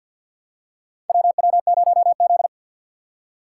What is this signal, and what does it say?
Morse code, heard as a keyed tone, about 51 wpm, 700 Hz. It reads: WW0Z
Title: USA Call Signs - 4 Characters - 2x Character Spacing - 50wpm